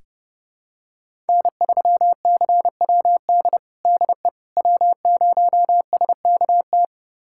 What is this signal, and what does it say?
Morse code, heard as a keyed tone, about 30 wpm, 700 Hz. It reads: N3CWD DE W0SKT
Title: USA Callsign DE Callsign - 5 Characters - 30wpm